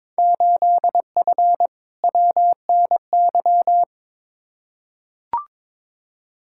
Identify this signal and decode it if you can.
Morse code, heard as a keyed tone, about 22 wpm, 700 Hz. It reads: 8F WNY
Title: Field Day Exchanges Encore - 22wpm